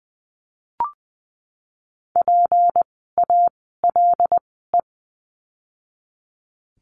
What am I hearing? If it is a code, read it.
Morse code, heard as a keyed tone, about 20 wpm, spaced as for 14 wpm, 700 Hz. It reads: PALE
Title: 153 Lesson 25f - Letter p - Words 20wpm